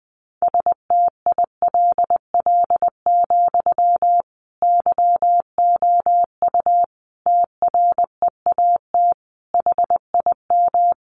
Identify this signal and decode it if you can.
Morse code, heard as a keyed tone, about 20 wpm, 700 Hz. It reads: STILL, YOU TREAT HIM
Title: Sentences from Top 1300 Words 20wpm